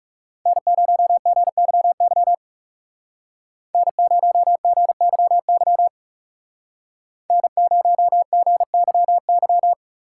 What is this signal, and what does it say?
Morse code, heard as a keyed tone, about 35 wpm, 700 Hz. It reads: N0GYY N0GYY N0GYY
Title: Speed Racing - USA 1x3 Call Signs 30wpm